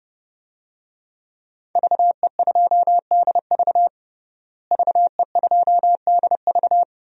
Morse code, heard as a keyed tone, about 30 wpm, 700 Hz. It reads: VE2DV VE2DV
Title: DXCC-100 Call Signs - 5 Characters - Repeated 2x - 30wpm